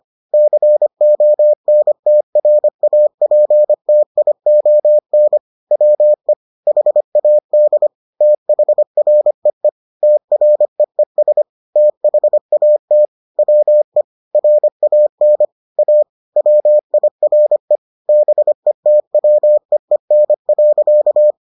Morse code, heard as a keyed tone, about 25 wpm, 600 Hz. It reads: CONTRAPTION WE HAD THREE TREES THAT WE RAN A WIRE BETWEEN.